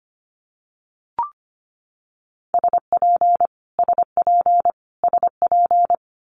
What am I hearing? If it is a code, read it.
Morse code, heard as a keyed tone, about 25 wpm, 700 Hz. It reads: SP SP SP
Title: Multi-letter Phonemes Repeated 3x 25wpm